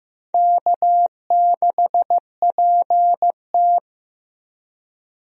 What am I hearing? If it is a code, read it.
Morse code, heard as a keyed tone, about 15 wpm, 700 Hz. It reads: K6PT
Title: USA Call Signs - 4 Characters - 15wpm